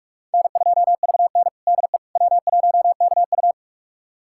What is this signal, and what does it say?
Morse code, heard as a keyed tone, about 42 wpm, 700 Hz. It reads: N2VN DE W1KU
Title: USA Callsign DE Callsign 45wpm